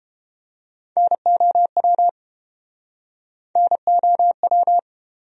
Morse code, heard as a keyed tone, about 33 wpm, 700 Hz. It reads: NOW NOW